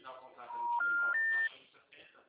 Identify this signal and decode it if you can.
Special information tone, with speech.